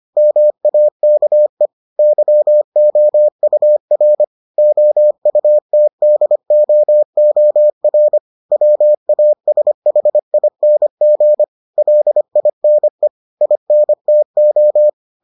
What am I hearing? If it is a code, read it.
Morse code, heard as a keyed tone, about 25 wpm, 600 Hz. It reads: MAKE YOUR OUTDOOR WASHING LINE INTO